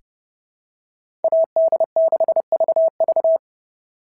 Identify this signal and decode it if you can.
Morse code, heard as a keyed tone, about 30 wpm, 660 Hz. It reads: AD6VV
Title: SOTA QSO Practice 30wpm